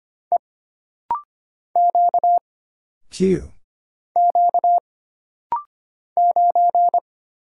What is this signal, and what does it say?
Morse code, heard as a keyed tone, about 25 wpm, 700 Hz. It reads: E Q Q 9
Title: Single Letter-Number Alt - Rapid-Fire 25wpm